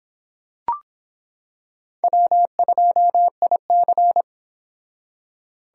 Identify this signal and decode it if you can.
Morse code, heard as a keyed tone, about 26 wpm, 700 Hz. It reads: W2IC